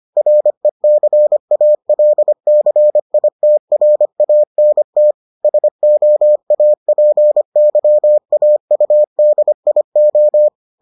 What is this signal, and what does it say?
Morse code, heard as a keyed tone, about 25 wpm, 600 Hz. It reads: RECALCITRANT SOAPYAUDIO